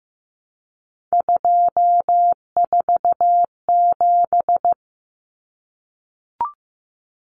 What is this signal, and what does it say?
Morse code, heard as a keyed tone, about 15 wpm, 700 Hz. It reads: 247
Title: Numbers - 3-Digit 15wpm